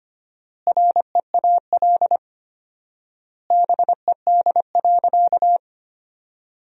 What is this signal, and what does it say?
Morse code, heard as a keyed tone, about 25 wpm, 700 Hz. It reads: REAL BED.
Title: 4x Word-Spacing - Sentences from Top 500 Words - 25wpm